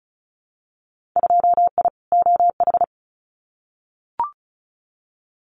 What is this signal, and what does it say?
Morse code, heard as a keyed tone, about 34 wpm, 700 Hz. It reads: 2I OH
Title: Winter Field Day Exchanges 35wpm